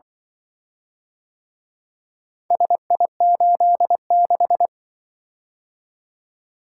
Morse code, heard as a keyed tone, about 24 wpm, 700 Hz. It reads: SI86
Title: Speed-Racing - Four Letter-Number Combinations - 20wpm